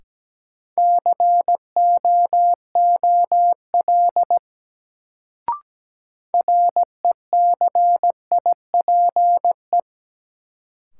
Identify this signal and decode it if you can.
Morse code, heard as a keyed tone, about 17 wpm, 700 Hz. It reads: COOL RECIPE
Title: Sentences from Top 1200 Words - New Word Drill Alt - 17wpm